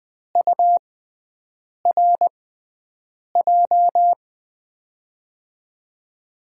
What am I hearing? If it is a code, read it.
Morse code, heard as a keyed tone, about 20 wpm, spaced as for 7 wpm, 700 Hz. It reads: URJ
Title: USA Call Signs - 6x Character Spacing - 20wpm